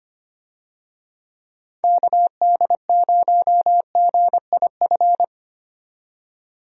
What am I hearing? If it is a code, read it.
Morse code, heard as a keyed tone, about 25 wpm, 700 Hz. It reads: KD0GIF
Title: USA Call Signs - 6 Characters - 25wpm